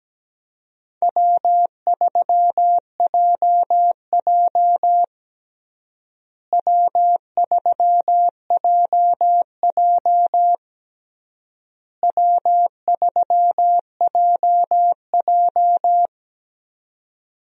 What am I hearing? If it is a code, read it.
Morse code, heard as a keyed tone, about 17 wpm, 700 Hz. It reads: W3JJ W3JJ W3JJ
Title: USA Call Signs - 4 Characters - Repeated 3x - 17wpm